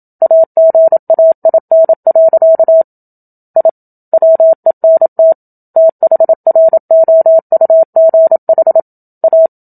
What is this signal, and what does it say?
Morse code, heard as a keyed tone, about 27 wpm, 650 Hz. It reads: AGAIN. I WENT THROUGH A
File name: Book Two / 01-Chapter 1 - 27 WPM